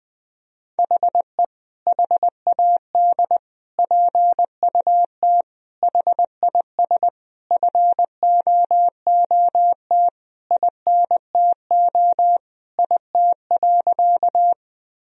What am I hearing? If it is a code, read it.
Morse code, heard as a keyed tone, about 20 wpm, 700 Hz. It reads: HE HAD PUT HIS FOOT INTO IT.
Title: Sentences from Top 1000 Words - Review Part 3 - 20wpm